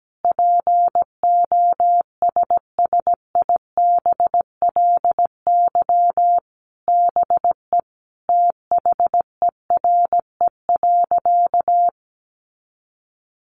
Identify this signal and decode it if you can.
Morse code, heard as a keyed tone, about 17 wpm, 700 Hz. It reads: POSSIBLY BE THERE.
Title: Sentences from Top 1300 Words 17wpm